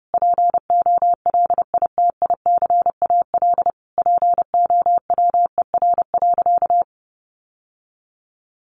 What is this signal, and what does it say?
Morse code, heard as a keyed tone, about 30 wpm, 700 Hz. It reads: POLITICAL POWER.
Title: Sentences from Top 900 Words 30wpm